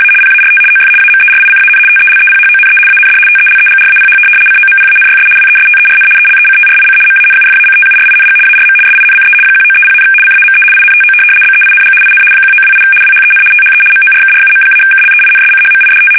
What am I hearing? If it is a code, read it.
RTTY, about 75 baud. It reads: A PART OF AMERICAN TELEPHONE AND TELEGRAPH COMPANYS WESTERN ELECTRIC MANUFACTURING ARM SINCE 1930, CAME INTO BEING IN 1928 WHEN THE MORKRUM-KLEINSCHMIDT
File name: fsk850-75B / teletype-corp